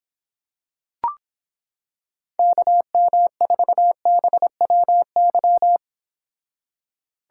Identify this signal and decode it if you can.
Morse code, heard as a keyed tone, about 26 wpm, 700 Hz. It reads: KM4BWY